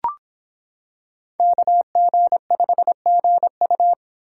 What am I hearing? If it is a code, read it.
Morse code, heard as a keyed tone, about 26 wpm, 700 Hz. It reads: KG5GU